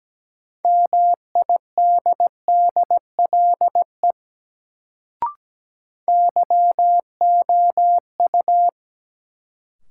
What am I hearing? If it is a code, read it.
Morse code, heard as a keyed tone, about 17 wpm, 700 Hz. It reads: MIDDLE YOU
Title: Sets of 2 Words - New Word Drill - 17wpm